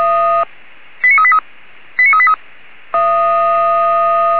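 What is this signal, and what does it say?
Five-tone selective call (ZVEI), heard as quick runs of five tones, tones of about 70 ms. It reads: 89383 89383
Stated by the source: Five-tone ZVEI selective call received by SDR.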